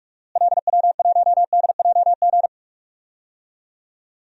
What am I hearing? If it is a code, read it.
Morse code, heard as a keyed tone, about 46 wpm, 700 Hz. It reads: LW1DJG
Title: DXCC-100 Call Signs - 6 Characters - Repeated 2x - 45wpm